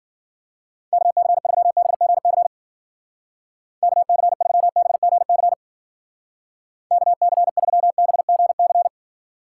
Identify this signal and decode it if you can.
Morse code, heard as a keyed tone, about 48 wpm, 700 Hz. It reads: KC3BGC KC3BGC KC3BGC
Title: Speed Racing - USA 2x3 Call Signs 40wpm